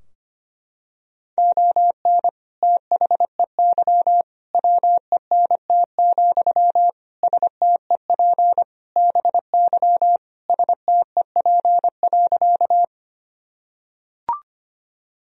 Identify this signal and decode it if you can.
Morse code, heard as a keyed tone, about 25 wpm, 700 Hz. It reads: ON THEY WENT, STEP BY STEP.
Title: Sentences from Top 1000 Words - Review Part 2 - 25wpm